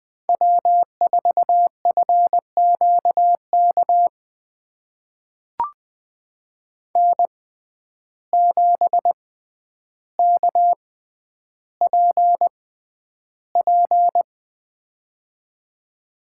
Morse code, heard as a keyed tone, about 20 wpm, spaced as for 10 wpm, 700 Hz. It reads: W4FQK N7KPP
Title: USA Call Signs - 5 Characters - 6x Character Spacing - 20wpm